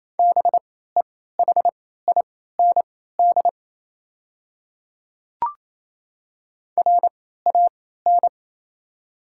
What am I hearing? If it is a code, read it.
Morse code, heard as a keyed tone, about 28 wpm, spaced as for 16 wpm, 690 Hz. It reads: BEHIND RAN
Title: 3x Character Spacing - Top 500 Words - 28wpm